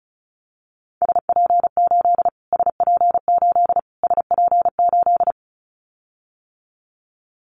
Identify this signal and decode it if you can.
Morse code, heard as a keyed tone, about 34 wpm, 700 Hz. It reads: SP8 SP8 SP8
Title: Callsign Prefixes for Top 25 DXCC - Repeated 3x - 35wpm